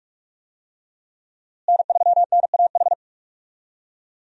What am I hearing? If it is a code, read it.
Morse code, heard as a keyed tone, about 41 wpm, 700 Hz. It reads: N3NAH
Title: USA Call Signs - 5 Characters - Repeated 2x - 45wpm